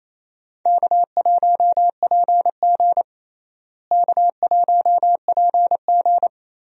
Morse code, heard as a keyed tone, about 28 wpm, 700 Hz. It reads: K1PG K1PG
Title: USA Call Signs - 4 Characters - Repeated 3x - 28wpm